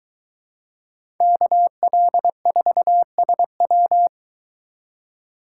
Morse code, heard as a keyed tone, about 22 wpm, 700 Hz. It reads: KL4SW